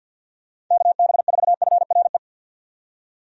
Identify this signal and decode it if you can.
Morse code, heard as a keyed tone, about 52 wpm, 700 Hz. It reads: KB4FRE
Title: USA Call Signs - 6 Characters - 50wpm